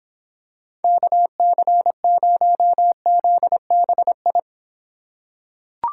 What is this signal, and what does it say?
Morse code, heard as a keyed tone, about 26 wpm, 700 Hz. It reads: KC0ZBI